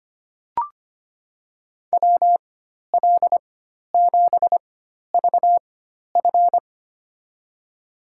Morse code, heard as a keyed tone, about 25 wpm, spaced as for 12 wpm, 700 Hz. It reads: WL7VF
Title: USA Call Signs - 5 Characters - 4x Character Spacing - 25wpm